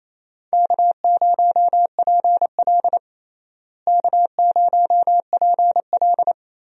Morse code, heard as a keyed tone, about 28 wpm, 700 Hz. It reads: K0PL K0PL
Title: USA Call Signs - 4 Characters - Repeated 3x - 28wpm